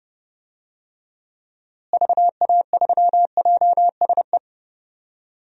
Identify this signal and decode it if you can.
Morse code, heard as a keyed tone, about 30 wpm, 700 Hz. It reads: VA3JSE
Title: DXCC-100 Call Signs - 6 Characters - Repeated 3x - 30wpm